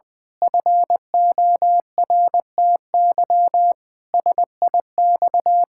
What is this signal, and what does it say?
Morse code, heard as a keyed tone, about 20 wpm, 700 Hz. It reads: FORTY SIX
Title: Pangram Challenge v2 - 20wpm